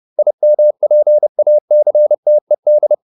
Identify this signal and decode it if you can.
Morse code, heard as a keyed tone, about 30 wpm, 600 Hz. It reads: IMPACTED
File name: IM8 30